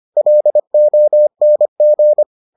Morse code, heard as a keyed tone, about 25 wpm, 600 Hz. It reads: LONG